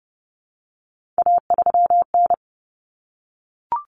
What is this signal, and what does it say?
Morse code, heard as a keyed tone, about 30 wpm, 700 Hz. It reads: A3N
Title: Common Cut Numbers - 3-Digit 30wpm